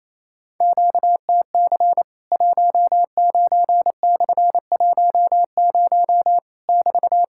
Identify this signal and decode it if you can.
Morse code, heard as a keyed tone, about 28 wpm, 700 Hz. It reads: QTC 19/10 <BT>
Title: Worked All Europe DX Contest - QTC Exchange - 28wpm